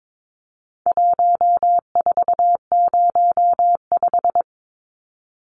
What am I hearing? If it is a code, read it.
Morse code, heard as a keyed tone, about 22 wpm, 700 Hz. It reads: 1405
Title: Numbers - 4-Digit 22wpm